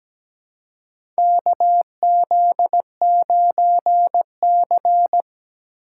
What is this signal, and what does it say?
Morse code, heard as a keyed tone, about 17 wpm, 700 Hz. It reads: KZ9C
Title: USA Call Signs - 4 Characters - 2x Character Spacing - 17wpm